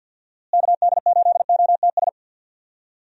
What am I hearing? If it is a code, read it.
Morse code, heard as a keyed tone, about 49 wpm, 700 Hz. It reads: KD8OTS
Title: USA Call Signs - 6 Characters - Repeated 3x - 50wpm